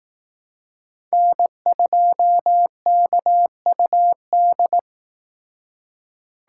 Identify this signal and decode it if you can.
Morse code, heard as a keyed tone, about 18 wpm, 700 Hz. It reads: N2KUD
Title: Speed Racing - USA 1x3 Call Signs 15wpm